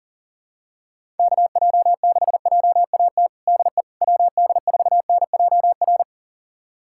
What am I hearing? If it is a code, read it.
Morse code, heard as a keyed tone, about 40 wpm, 700 Hz. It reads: KJ6JAT DE WD4NJR
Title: USA Callsign DE Callsign - 6 Characters - 40wpm